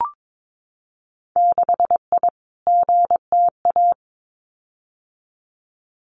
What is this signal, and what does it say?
Morse code, heard as a keyed tone, about 22 wpm, 700 Hz. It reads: 6I GTA
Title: Winter Field Day Exchanges Encore 22wpm